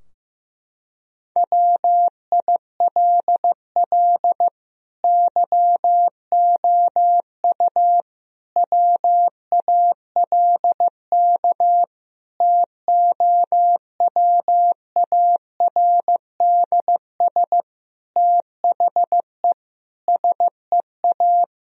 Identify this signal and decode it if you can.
Morse code, heard as a keyed tone, about 15 wpm, 700 Hz. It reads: WILL YOU WALK TOWARDS THE SEA
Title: Sentences from Top 1000 Words - Review Part 2 - 15wpm